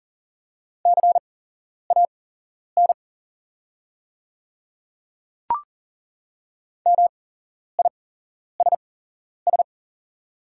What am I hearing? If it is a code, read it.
Morse code, heard as a keyed tone, about 39 wpm, spaced as for 11 wpm, 690 Hz. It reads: CAN MISS
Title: 8x Character Spacing - Top 500 Words - 40wpm